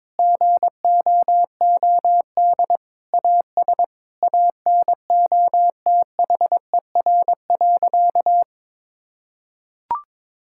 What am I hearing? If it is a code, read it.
Morse code, heard as a keyed tone, about 22 wpm, 700 Hz. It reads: GOOD AS ANOTHER.
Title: Sentences from Top 1600 Words 22wpm